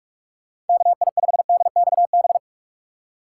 Morse code, heard as a keyed tone, about 46 wpm, 700 Hz. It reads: KI5DXB
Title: USA Call Signs - 6 Characters - 3x Character Spacing - 45wpm